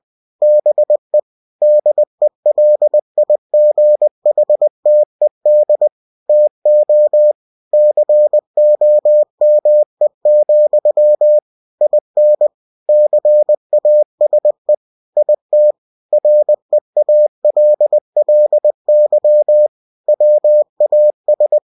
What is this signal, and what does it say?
Morse code, heard as a keyed tone, about 20 wpm, 600 Hz. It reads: BE DELIGHTED TO COME, IN CASE IT REALLY WAS
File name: cw-books / winnie-the-pooh / winnie-20wpm-page-0022